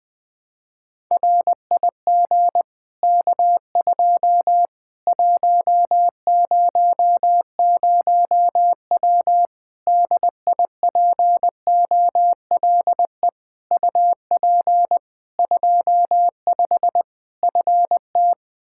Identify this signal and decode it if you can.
Morse code, heard as a keyed tone, about 20 wpm, 700 Hz. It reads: RIG K2 100W DIPOLE UP 25 FT
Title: Practice QSO - Exchange 2 - Info - 20wpm